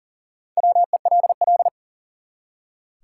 Morse code, heard as a keyed tone, about 40 wpm, 700 Hz. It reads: WELL
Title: Top 180 QSO Elements 40wpm